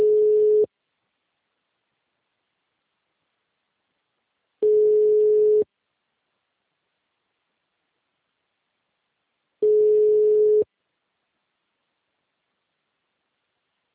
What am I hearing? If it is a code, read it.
Ringback tone (Europe, 425 Hz).
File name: out- -2050- - - .81112